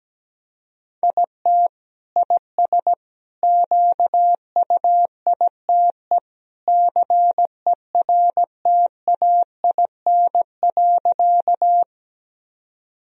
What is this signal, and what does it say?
Morse code, heard as a keyed tone, about 17 wpm, 700 Hz. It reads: IT IS QUITE CERTAIN.
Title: Sentences from Top 1000 Words - Review Part 2 - 17wpm